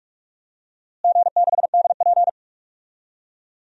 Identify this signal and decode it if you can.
Morse code, heard as a keyed tone, about 46 wpm, 700 Hz. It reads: G6DP
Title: DXCC-100 Call Signs - 4 Characters - Repeated 2x - 45wpm